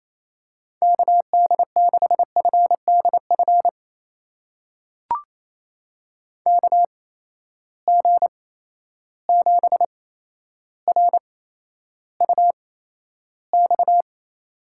Morse code, heard as a keyed tone, about 28 wpm, spaced as for 12 wpm, 700 Hz. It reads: KD6FDF KG7RUX
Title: USA Call Signs - 6 Characters - 8x Character Spacing - 28wpm